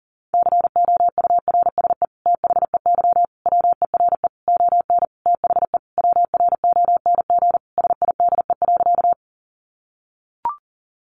Morse code, heard as a keyed tone, about 40 wpm, 700 Hz. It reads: COURSE THEY WERE ON THE WRONG SIDE.
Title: Sentences from Top 900 Words 40wpm